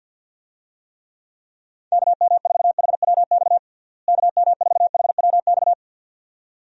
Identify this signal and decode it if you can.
Morse code, heard as a keyed tone, about 51 wpm, 700 Hz. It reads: KM4HWX KM4HWX
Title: USA Call Signs - 6 Characters - Repeated 2x - 50wpm